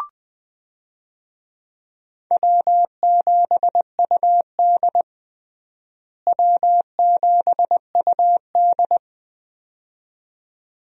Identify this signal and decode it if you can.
Morse code, heard as a keyed tone, about 20 wpm, 700 Hz. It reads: W7UD W7UD
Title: USA Call Signs - 4 Characters - Repeated 2x - 20wpm